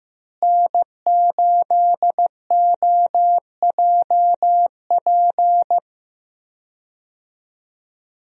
Morse code, heard as a keyed tone, about 15 wpm, 700 Hz. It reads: N8OJP
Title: USA Call Signs - 5 Characters - Repeated 3x - 15wpm